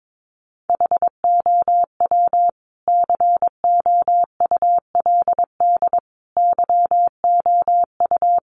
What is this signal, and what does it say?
Morse code, heard as a keyed tone, about 22 wpm, 700 Hz. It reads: HOW COULD YOU
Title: Sentences from Top 1300 Words 22wpm